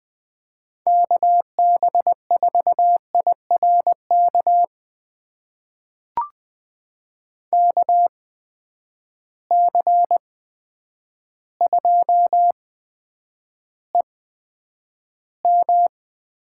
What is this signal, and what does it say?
Morse code, heard as a keyed tone, about 20 wpm, spaced as for 9 wpm, 700 Hz. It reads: KB4IRK KC2EM
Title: USA Call Signs - 6 Characters - 8x Character Spacing - 20wpm